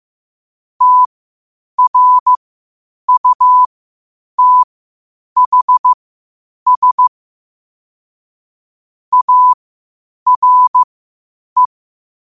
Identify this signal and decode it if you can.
Morse code, heard as a keyed tone, about 15 wpm, spaced as for 8 wpm, 1000 Hz. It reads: TRUTHS ARE
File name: qotdfiles / 8wpm